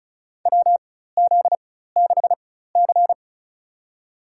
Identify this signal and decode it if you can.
Morse code, heard as a keyed tone, about 36 wpm, spaced as for 16 wpm, 700 Hz. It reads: WZ6C
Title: USA Call Signs - 4 Characters - 4x Character Spacing - 35wpm